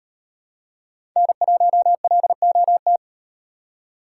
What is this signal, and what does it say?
Morse code, heard as a keyed tone, about 38 wpm, 700 Hz. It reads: N1LOT